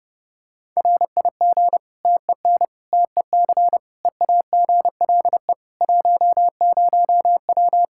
Morse code, heard as a keyed tone, about 32 wpm, 700 Hz. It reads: RIG TEN TEC EAGLE 10W
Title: Practice QSO - Exchange 2 - Info - 30wpm